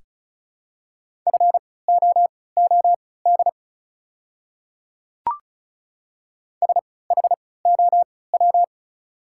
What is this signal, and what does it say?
Morse code, heard as a keyed tone, about 34 wpm, spaced as for 20 wpm, 690 Hz. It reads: FOOD SHOW
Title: 3x Character Spacing - Top 500 Words - 35wpm